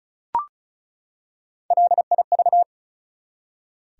Morse code, heard as a keyed tone, about 36 wpm, 700 Hz. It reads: LIV
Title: USA Common Names - Max 4 Letters - 35wpm